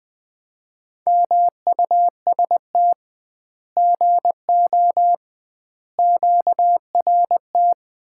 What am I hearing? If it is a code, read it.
Morse code, heard as a keyed tone, about 21 wpm, 700 Hz. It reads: MUST GO QRT
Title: Common QSO Phrases - 2x Word-Spacing - 20wpm